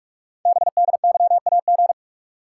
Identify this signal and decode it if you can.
Morse code, heard as a keyed tone, about 44 wpm, 690 Hz. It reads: BDYAG
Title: Code Groups - 5 Letters - 45wpm